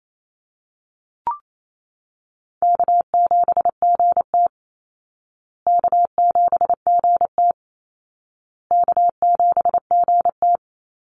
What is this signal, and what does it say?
Morse code, heard as a keyed tone, about 28 wpm, 700 Hz. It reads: K7GT K7GT K7GT
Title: Easy Call Signs Repeated 3x - 28wpm